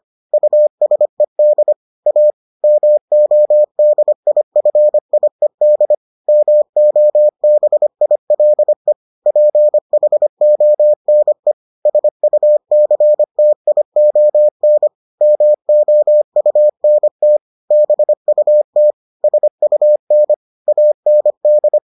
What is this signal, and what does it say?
Morse code, heard as a keyed tone, about 25 wpm, 600 Hz. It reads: USED A MODIFIED MOBILE PHONE SUCTION MOUNT BUT SUN AND